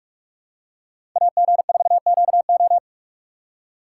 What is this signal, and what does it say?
Morse code, heard as a keyed tone, about 46 wpm, 700 Hz. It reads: AG4QO
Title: USA Call Signs - 5 Characters - 3x Character Spacing - 45wpm